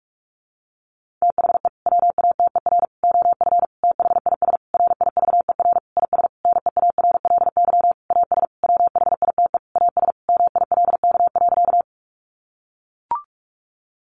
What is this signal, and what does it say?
Morse code, heard as a keyed tone, about 44 wpm, 700 Hz. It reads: THE WATER OF THIS RIVER IS NEARLY AS WHITE AS MILK.
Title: Sentences from Top 1500 Words 45wpm